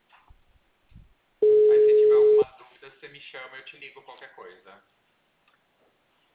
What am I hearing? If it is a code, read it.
Ringback tone (Europe, 425 Hz), with speech.
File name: out- -2007- - - .1594